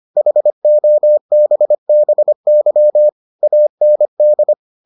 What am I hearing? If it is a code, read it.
Morse code, heard as a keyed tone, about 25 wpm, 600 Hz. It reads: HOBBY AND